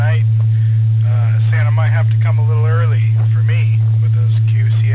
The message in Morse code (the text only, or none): none